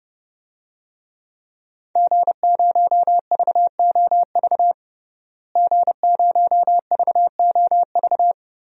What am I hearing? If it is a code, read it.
Morse code, heard as a keyed tone, about 30 wpm, 700 Hz. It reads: G0VOV G0VOV
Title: DXCC-100 Call Signs - 5 Characters - Repeated 2x - 30wpm